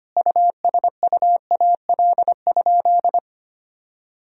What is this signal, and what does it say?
Morse code, heard as a keyed tone, about 25 wpm, 700 Hz. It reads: USUAL?
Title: Sentences from Top 1000 Words - Review Part 2 - 25wpm